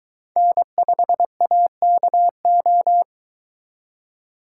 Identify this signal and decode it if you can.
Morse code, heard as a keyed tone, about 23 wpm, 700 Hz. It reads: N5AKO